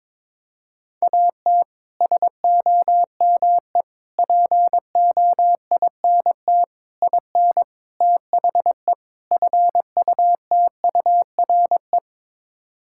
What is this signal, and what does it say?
Morse code, heard as a keyed tone, about 22 wpm, 700 Hz. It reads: AT SOME POINT IN THE FUTURE
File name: practice / 22wpm / Sets of 6 Words - 22wpm